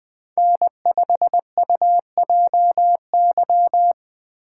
Morse code, heard as a keyed tone, about 20 wpm, 700 Hz. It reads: N5UJY
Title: Speed Racing - USA 1x3 Call Signs 17wpm